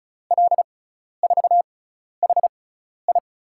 Morse code, heard as a keyed tone, about 36 wpm, spaced as for 12 wpm, 700 Hz. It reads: L4HI
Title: USA Call Signs - 5 Characters - 6x Character Spacing - 35wpm